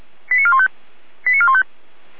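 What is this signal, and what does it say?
Five-tone selective call (ZVEI), heard as quick runs of five tones, tones of about 70 ms. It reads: 89415 89415
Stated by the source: ZVEI five-tone selective call 89220, received on SDR, from the Feuerwehr Orlamünde audio server.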